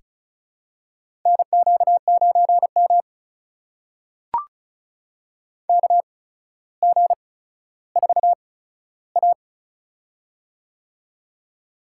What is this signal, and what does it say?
Morse code, heard as a keyed tone, about 36 wpm, spaced as for 15 wpm, 700 Hz. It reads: NQ9M KG4A
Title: USA Call Signs - 4 Characters - 8x Character Spacing - 35wpm